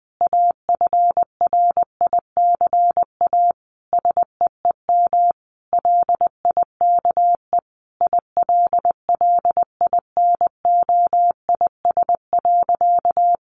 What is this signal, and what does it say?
Morse code, heard as a keyed tone, about 20 wpm, 700 Hz. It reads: AFRICA SEEM LIKE ILLINOIS.
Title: A Sound Of Thunder 20wpm